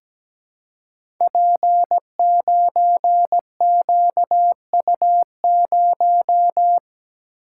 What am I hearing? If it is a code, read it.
Morse code, heard as a keyed tone, about 17 wpm, 700 Hz. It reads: P9QU0
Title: Code Groups - 5 Letter:Numbers - 17wpm